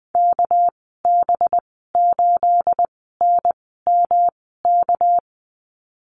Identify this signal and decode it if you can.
Morse code, heard as a keyed tone, about 20 wpm, spaced as for 15 wpm, 700 Hz. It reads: KB8NMK
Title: 245 Lesson 39g - Letter z - Call Signs 20wpm